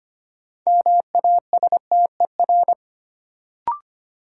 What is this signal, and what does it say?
Morse code, heard as a keyed tone, about 25 wpm, 700 Hz. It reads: MASTER
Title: Sentences from Top 1300 Words - New Word Drill - 25wpm